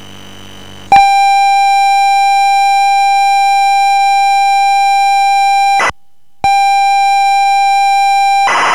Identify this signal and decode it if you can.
Computer data on cassette.